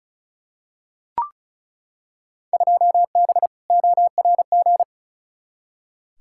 Morse code, heard as a keyed tone, about 36 wpm, 690 Hz. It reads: 2B ORG
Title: Field Day Exchanges Encore - 35wpm